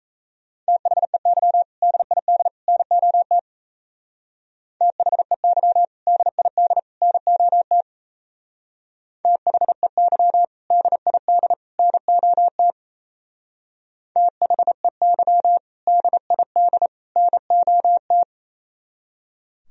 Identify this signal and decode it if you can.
Morse code, heard as a keyed tone, about 32 wpm, 700 Hz. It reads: THEY DID NOT THEY DID NOT THEY DID NOT THEY DID NOT
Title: Speed-Racing - Sets of 3 Words 28wpm